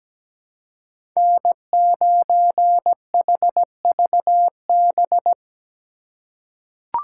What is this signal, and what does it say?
Morse code, heard as a keyed tone, about 17 wpm, 700 Hz. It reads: N9HVB
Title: USA Call Signs - 5 Characters - 4x Character Spacing - 17wpm